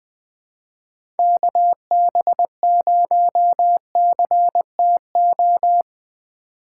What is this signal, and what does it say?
Morse code, heard as a keyed tone, about 21 wpm, 700 Hz. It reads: KB0CTO
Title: USA Call Signs - 6 Characters - 2x Character Spacing - 20wpm